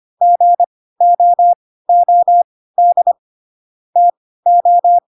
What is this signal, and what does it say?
Morse code, heard as a keyed tone, about 25 wpm, spaced as for 16 wpm, 700 Hz. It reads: GOOD TO